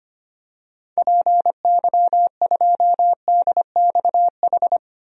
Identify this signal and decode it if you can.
Morse code, heard as a keyed tone, about 25 wpm, 700 Hz. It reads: PY2DXH
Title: DXCC-100 Call Signs - 6 Characters - Repeated 3x - 25wpm